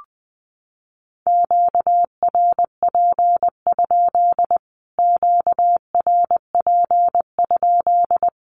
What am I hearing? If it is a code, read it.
Morse code, heard as a keyed tone, about 20 wpm, 700 Hz. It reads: QRP? QRP?
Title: Q-Codes Repeated 3x 20wpm